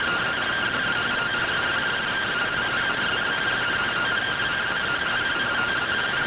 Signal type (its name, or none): two-tone teleprinter signal